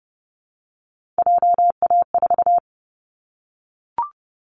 Morse code, heard as a keyed tone, about 30 wpm, 700 Hz. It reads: JA4
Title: Callsign Prefixes for Top 25 DXCC - Repeated 3x - 30wpm